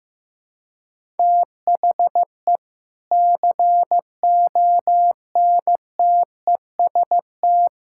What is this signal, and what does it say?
Morse code, heard as a keyed tone, about 16 wpm, 700 Hz. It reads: THE CONTEST
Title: Sentences from Top 1600 Words 15wpm